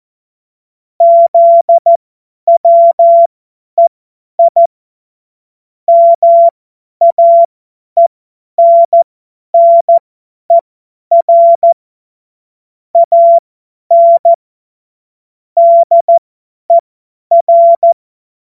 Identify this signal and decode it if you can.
Morse code, heard as a keyed tone, about 14 wpm, spaced as for 10 wpm, 680 Hz. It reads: ZWEI MAENNER AN DER
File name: Morsekurs / Witze / Witz 30. August 2022 50 F70